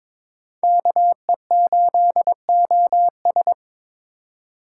Morse code, heard as a keyed tone, about 26 wpm, 700 Hz. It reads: KE8OS
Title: Speed Racing - USA 2x2 Call Signs 22wpm